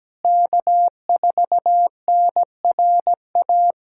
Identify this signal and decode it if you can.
Morse code, heard as a keyed tone, about 17 wpm, 700 Hz. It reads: K4NRA
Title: K1USN SST QSO Practice - 17wpm